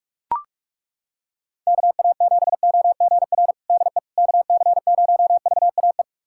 Morse code, heard as a keyed tone, about 42 wpm, 700 Hz. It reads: KA7OGR DE KC0VAE
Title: USA Callsign DE Callsign - 6 Characters - 45wpm